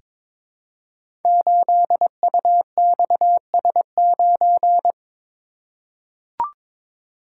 Morse code, heard as a keyed tone, about 22 wpm, 700 Hz. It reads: 8UXS9
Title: Code Groups - 5 Letter:Numbers - 22wpm